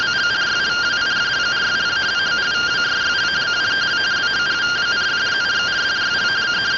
Two-tone teleprinter signal, about 170 Hz shift.